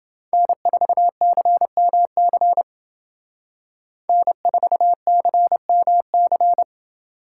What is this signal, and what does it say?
Morse code, heard as a keyed tone, about 28 wpm, 700 Hz. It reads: N4CMC N4CMC